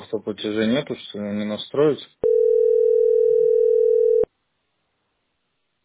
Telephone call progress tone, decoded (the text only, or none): ringback tone